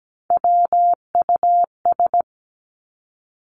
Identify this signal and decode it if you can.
Morse code, heard as a keyed tone, about 17 wpm, 700 Hz. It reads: WUS
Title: Three Challenging Letter-Number Combinations 17wpm